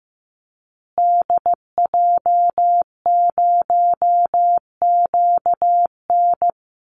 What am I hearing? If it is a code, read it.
Morse code, heard as a keyed tone, about 15 wpm, 700 Hz. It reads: DJ0QN
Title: Call Signs Repeated 2x - 15wpm